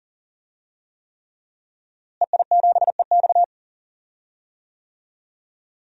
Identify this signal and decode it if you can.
Morse code, heard as a keyed tone, about 40 wpm, 700 Hz. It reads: EI7EX
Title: DXCC-100 Call Signs - 5 Characters - Repeated 3x - 40wpm